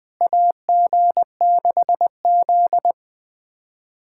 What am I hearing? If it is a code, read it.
Morse code, heard as a keyed tone, about 20 wpm, 700 Hz. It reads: AG6Z
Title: USA Callsign DE Callsign - 4 Characters - 20wpm